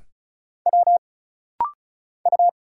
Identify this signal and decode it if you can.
Morse code, heard as a keyed tone, about 34 wpm, 700 Hz. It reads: W U
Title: Single Letters - Rapid-Fire 35wpm